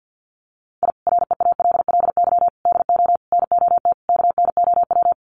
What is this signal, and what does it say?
Morse code, heard as a keyed tone, about 50 wpm, 700 Hz. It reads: I REALLY DO NOT KNOW
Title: Sets of 5 Words Encore 50wpm